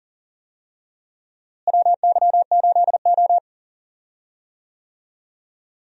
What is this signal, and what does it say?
Morse code, heard as a keyed tone, about 40 wpm, 700 Hz. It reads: WY8O
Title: USA Call Signs - 4 Characters - Repeated 3x - 40wpm